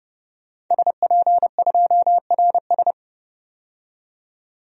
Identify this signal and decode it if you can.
Morse code, heard as a keyed tone, about 30 wpm, 700 Hz. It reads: SP2RS
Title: DXCC-100 Call Signs - 5 Characters - 30wpm